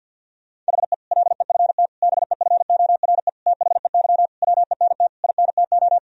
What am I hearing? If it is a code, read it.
Morse code, heard as a keyed tone, about 50 wpm, 700 Hz. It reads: HE LEFT BEFORE THEY WENT INTO
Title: Sentences from Top 1000 Words - Review Part 4 - 50wpm